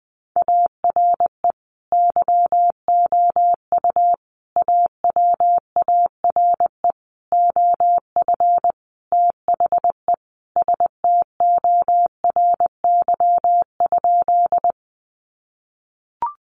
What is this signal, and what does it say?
Morse code, heard as a keyed tone, about 20 wpm, 700 Hz. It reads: ARE YOU AWARE OF THE STORY?
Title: Sentences from Top 1200 Words 20wpm